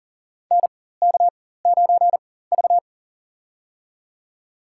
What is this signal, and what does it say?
Morse code, heard as a keyed tone, about 40 wpm, spaced as for 19 wpm, 700 Hz. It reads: NK9V
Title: USA Call Signs - 4 Characters - 4x Character Spacing - 40wpm